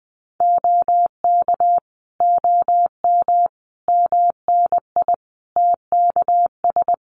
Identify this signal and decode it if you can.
Morse code, heard as a keyed tone, about 20 wpm, 700 Hz. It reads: OK OM MNI TKS
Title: Common QSO Phrases 20wpm